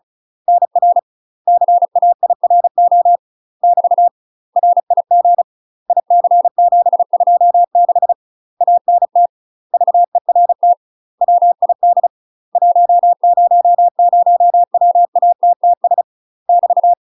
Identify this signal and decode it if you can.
Morse code, heard as a keyed tone, about 36 wpm, 700 Hz. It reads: NR CAIRO <BT> RIG IC726 ANT VERT WID 100WATTS <BT>
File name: UK MASTER QSOs TRAINING / 35WPM / UK MASTER QSOs 0040